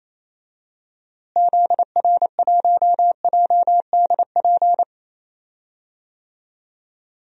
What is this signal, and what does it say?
Morse code, heard as a keyed tone, about 28 wpm, 700 Hz. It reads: ZR1JDP
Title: DXCC-100 Call Signs - 6 Characters - Repeated 2x - 28wpm